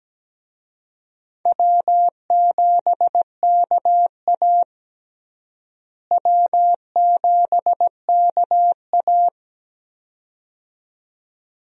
Morse code, heard as a keyed tone, about 17 wpm, 700 Hz. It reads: W7KA W7KA
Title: USA Call Signs - 4 Characters - Repeated 2x - 17wpm